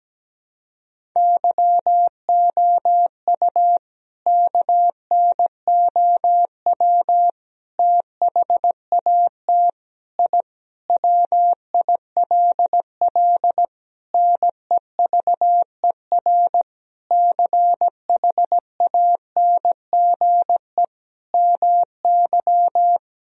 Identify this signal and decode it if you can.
Morse code, heard as a keyed tone, about 17 wpm, 700 Hz. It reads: YOU KNOW THAT I WILL NEVER CHANGE MY
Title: Sentences from Top 1000 Words - Review Part 1 - 17wpm